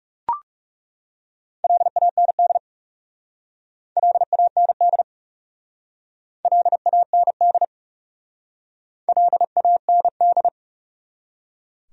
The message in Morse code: LAND LAND LAND LAND

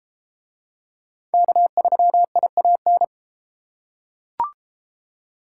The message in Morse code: K3IAN

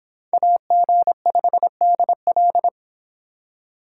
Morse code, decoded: AG5DL

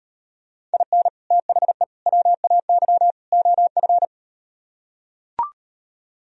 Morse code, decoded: IN THE WAY OF

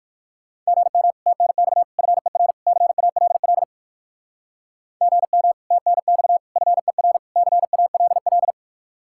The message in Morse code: GM TNX FER CALL GM TNX FER CALL

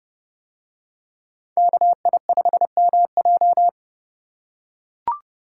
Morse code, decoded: KI5MJ

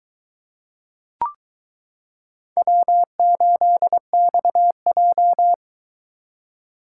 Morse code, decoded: W8XJ

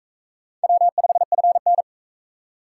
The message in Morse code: W5FN

